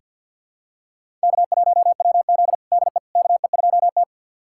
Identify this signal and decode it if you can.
Morse code, heard as a keyed tone, about 46 wpm, 700 Hz. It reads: K1WZ DE KE2T
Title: USA Callsign DE Callsign - 4 Characters - 50wpm